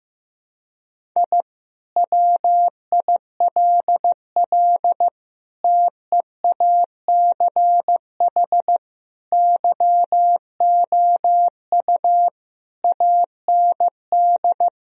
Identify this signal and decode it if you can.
Morse code, heard as a keyed tone, about 15 wpm, 700 Hz. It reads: I WILL TEACH YOU AND
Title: Sentences from Top 1000 Words - Review Part 3 - 15wpm